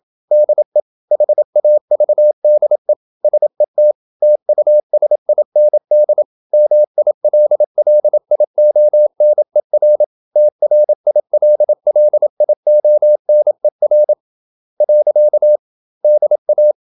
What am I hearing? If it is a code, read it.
Morse code, heard as a keyed tone, about 27 wpm, 600 Hz. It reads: DE HAVDE SET TUSIND MILLIONER TRILLIONER. DA
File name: Ferie QRQ 27wpm